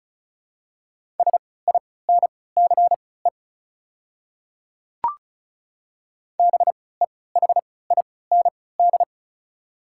Morse code, heard as a keyed tone, about 36 wpm, spaced as for 20 wpm, 700 Hz. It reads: SINCE BEHIND
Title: 3x Character Spacing - Top 500 Words - 35wpm